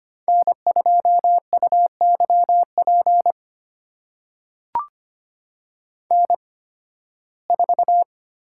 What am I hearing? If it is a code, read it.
Morse code, heard as a keyed tone, about 25 wpm, spaced as for 16 wpm, 700 Hz. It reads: N2UYP N4
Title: USA Call Signs - 5 Characters - 8x Character Spacing - 25wpm